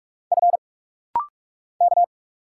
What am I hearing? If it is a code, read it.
Morse code, heard as a keyed tone, about 45 wpm, 700 Hz. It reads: F K
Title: Single Letters - Rapid-Fire 45wpm